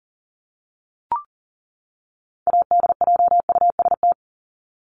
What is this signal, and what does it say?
Morse code, heard as a keyed tone, about 40 wpm, 700 Hz. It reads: ADJUST
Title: Top 1501-1750 Words 40wpm